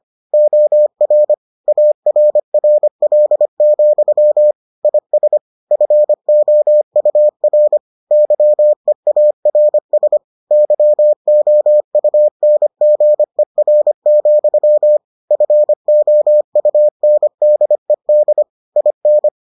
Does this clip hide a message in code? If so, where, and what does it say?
Yes: Morse code from 0.3 to 19.3 s, heard as a keyed tone, about 25 wpm, 600 Hz. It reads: OR ARRL, IS FOUR YEARS YOUNGER, FOUNDED IN